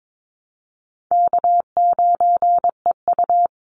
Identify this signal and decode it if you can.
Morse code, heard as a keyed tone, about 22 wpm, 700 Hz. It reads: K9EU
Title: Call Signs Repeated 3x - 22wpm